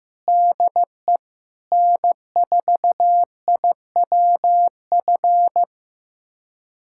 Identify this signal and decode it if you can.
Morse code, heard as a keyed tone, about 15 wpm, 700 Hz. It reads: DE N4IWF
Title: USA Callsign DE Callsign 15wpm